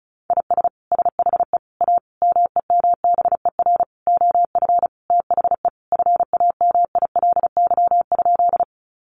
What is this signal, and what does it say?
Morse code, heard as a keyed tone, about 34 wpm, 700 Hz. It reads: IS SHE A MEMBER OF THE FAMILY?
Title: Sentences from Top 1100 Words 35wpm